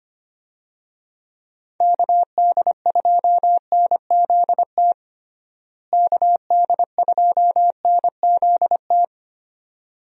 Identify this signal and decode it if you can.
Morse code, heard as a keyed tone, about 25 wpm, 700 Hz. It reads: KD2NZT KD2NZT
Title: USA Call Signs - 6 Characters - Repeated 2x - 25wpm